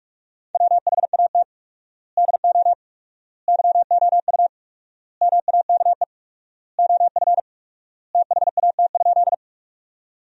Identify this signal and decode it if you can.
Morse code, heard as a keyed tone, about 41 wpm, 700 Hz. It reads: WHAT DO YOU MAKE OF THAT?
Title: 4x Word-Spacing - Sentences from Top 100 Words - 45wpm